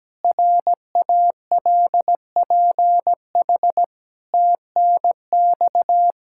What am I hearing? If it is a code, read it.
Morse code, heard as a keyed tone, about 17 wpm, 700 Hz. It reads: RALPH TNX
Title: Practice QSO - Exchange 4 - Ending - 17wpm